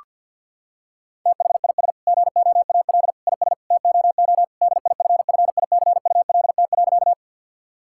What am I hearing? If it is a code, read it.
Morse code, heard as a keyed tone, about 50 wpm, 700 Hz. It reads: THIS GOAL IS TOO DIFFICULT.
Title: Sentences from Top 1000 Words - Review Part 5 - 50wpm